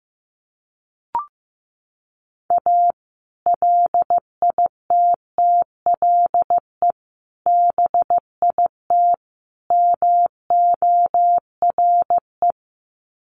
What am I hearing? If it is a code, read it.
Morse code, heard as a keyed tone, about 15 wpm, 700 Hz. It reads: A LITTLE BIT MORE
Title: Sets of 4 Words 15wpm